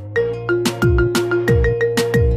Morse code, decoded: none